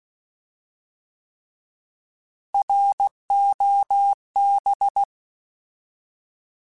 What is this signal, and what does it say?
Morse code, heard as a keyed tone, about 16 wpm, 800 Hz. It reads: ROB
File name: Alphabetical / Alphabetical mp3s / Rs @15